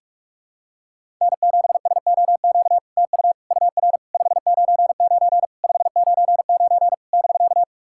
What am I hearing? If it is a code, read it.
Morse code, heard as a keyed tone, about 46 wpm, 700 Hz. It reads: N7SOQ TU UR 599 599 <BK>
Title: SOTA QSO Practice 45wpm